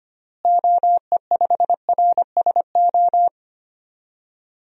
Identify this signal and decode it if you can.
Morse code, heard as a keyed tone, about 25 wpm, 700 Hz. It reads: OE5RSO
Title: DXCC-100 Call Signs - 6 Characters - Repeated 2x - 25wpm